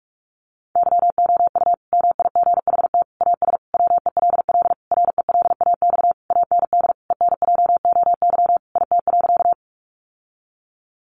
Morse code, heard as a keyed tone, about 44 wpm, 700 Hz. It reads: YOU MIGHT AS WELL RELAX AND ENJOY IT.
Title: Sentences from Top 1400 Words 45wpm